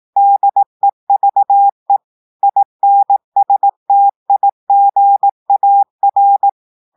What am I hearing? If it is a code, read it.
Morse code, heard as a keyed tone, about 18 wpm, 800 Hz. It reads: DEVE INSTIGAR
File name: audios / portugais / portugais texte en clair 2 vitesse 18 mots minutes